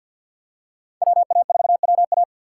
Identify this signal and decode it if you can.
Morse code, heard as a keyed tone, about 47 wpm, 700 Hz. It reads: WA4WA
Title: USA Call Signs - 5 Characters - 6x Character Spacing - 50wpm